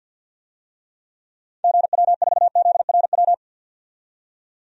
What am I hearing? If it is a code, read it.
Morse code, heard as a keyed tone, about 50 wpm, 700 Hz. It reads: GW4ZRW
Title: DXCC-100 Call Signs - 6 Characters - 50wpm